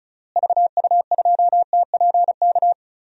Morse code, heard as a keyed tone, about 34 wpm, 700 Hz. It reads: VU2TPK
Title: DXCC-100 Call Signs - 6 Characters - Repeated 3x - 35wpm